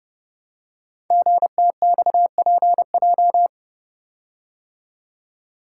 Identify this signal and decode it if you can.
Morse code, heard as a keyed tone, about 30 wpm, 690 Hz. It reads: GTXPJ
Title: Code Groups - 5 Letters - 30wpm